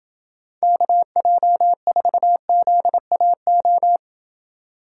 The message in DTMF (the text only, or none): none